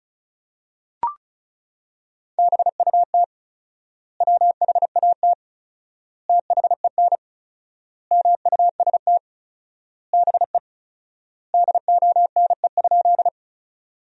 Morse code, heard as a keyed tone, about 36 wpm, 700 Hz. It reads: BUT WHAT THEN MUST BE DONE?
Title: 4x Word-Spacing - Sentences from Top 500 Words - 35wpm